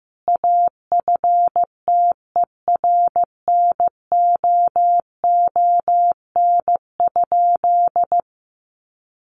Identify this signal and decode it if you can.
Morse code, heard as a keyed tone, about 14 wpm, 700 Hz. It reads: AFTERNOON?
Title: Sentences from Top 1500 Words 15wpm